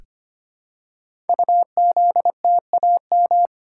Morse code, heard as a keyed tone, about 25 wpm, 700 Hz. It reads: UZTAM
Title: Code Groups - 5 Letters - 25wpm